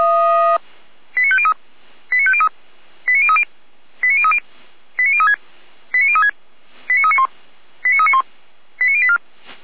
Five-tone selective call (ZVEI), heard as quick runs of five tones, tones of about 70 ms. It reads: 89503 89503 89030 89030 89036 89036 89391 89391 89084